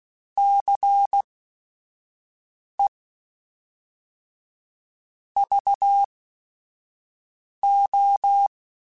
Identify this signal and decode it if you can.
Morse code, heard as a keyed tone, about 16 wpm, spaced as for 5 wpm, 800 Hz. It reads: CE VO
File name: Technical Subjects / Technical Subjects-mp3s / XC @5